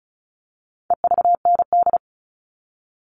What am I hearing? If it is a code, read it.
Morse code, heard as a keyed tone, about 34 wpm, 700 Hz. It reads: EVND
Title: All Cut Numbers - 4-Digit 35wpm